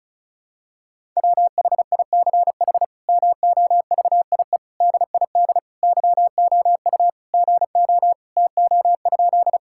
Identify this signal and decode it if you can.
Morse code, heard as a keyed tone, about 34 wpm, 700 Hz. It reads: WHICH MOVIE DID YOU GO TO?
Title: Sentences from Top 1000 Words - Review Part 5 - 35wpm